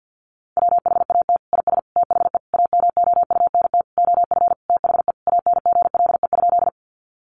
Morse code, heard as a keyed tone, about 49 wpm, 700 Hz. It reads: WHAT IS THE AMOUNT OF THE ANGLE?
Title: Sentences from Top 1500 Words 50wpm